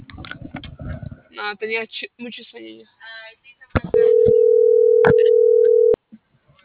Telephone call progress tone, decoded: ringback tone

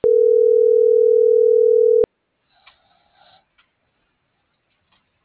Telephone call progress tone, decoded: ringback tone

